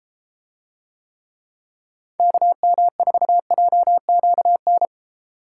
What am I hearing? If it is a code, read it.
Morse code, heard as a keyed tone, about 32 wpm, 700 Hz. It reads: KM4JQN